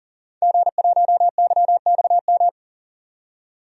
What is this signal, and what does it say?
Morse code, heard as a keyed tone, about 40 wpm, 700 Hz. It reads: G1YXM
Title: DXCC-100 Call Signs - 5 Characters - 40wpm